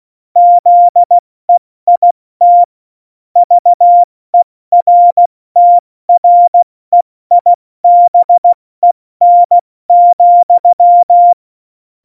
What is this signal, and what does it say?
Morse code, heard as a keyed tone, about 16 wpm, 700 Hz. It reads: ZEIT VERTREIBEN,
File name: Morsekurs / Witze / Witz 15. Januar 2019, 70 F80